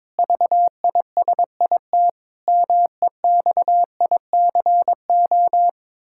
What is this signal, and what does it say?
Morse code, heard as a keyed tone, about 22 wpm, 700 Hz. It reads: VISIT MEXICO
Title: Pangram Challenge v2 - 22wpm